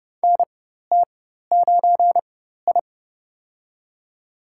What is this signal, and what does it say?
Morse code, heard as a keyed tone, about 30 wpm, spaced as for 14 wpm, 700 Hz. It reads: NT9I
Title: USA Call Signs - 4 Characters - 4x Character Spacing - 30wpm